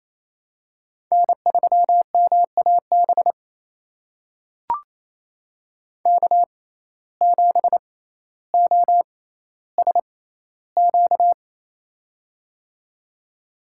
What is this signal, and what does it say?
Morse code, heard as a keyed tone, about 28 wpm, spaced as for 14 wpm, 700 Hz. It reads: N3MAB K7OSQ
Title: USA Call Signs - 5 Characters - 6x Character Spacing - 28wpm